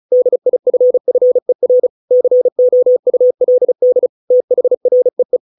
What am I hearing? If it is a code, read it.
Morse code, heard as a keyed tone, about 34 wpm, 500 Hz. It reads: DIFFER COULD THREE